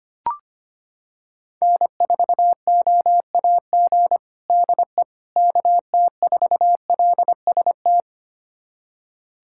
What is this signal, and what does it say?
Morse code, heard as a keyed tone, about 25 wpm, 700 Hz. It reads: N4OAG DE KT4LST
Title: USA Callsign DE Callsign 25wpm